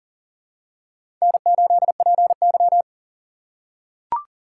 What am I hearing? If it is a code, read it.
Morse code, heard as a keyed tone, about 39 wpm, 700 Hz. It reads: N8PY
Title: USA Call Signs - 4 Characters - 2x Character Spacing - 40wpm